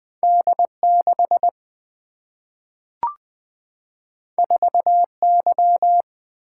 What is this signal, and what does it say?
Morse code, heard as a keyed tone, about 20 wpm, 700 Hz. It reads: D 6 4 Y
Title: Single-Letter Single-Number 20wpm